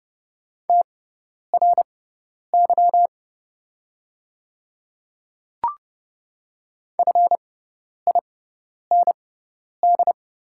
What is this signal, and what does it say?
Morse code, heard as a keyed tone, about 31 wpm, spaced as for 10 wpm, 700 Hz. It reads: TRY FIND
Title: 6x Character Spacing - Top 300 Words - 30wpm